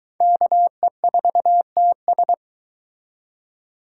Morse code, heard as a keyed tone, about 22 wpm, 700 Hz. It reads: KE4TS